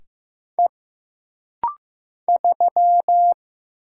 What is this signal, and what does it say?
Morse code, heard as a keyed tone, about 15 wpm, 700 Hz. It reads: E 3
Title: Single Letter-Number Alt - Rapid-Fire 15wpm